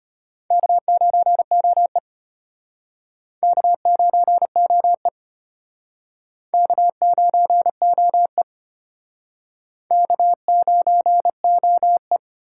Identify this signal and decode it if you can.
Morse code, heard as a keyed tone, about 33 wpm, 700 Hz. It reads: K9OE K9OE K9OE K9OE